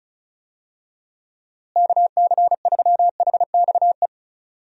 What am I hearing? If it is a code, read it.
Morse code, heard as a keyed tone, about 34 wpm, 700 Hz. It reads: KC3HXE
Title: USA Call Signs - 6 Characters - Repeated 3x - 35wpm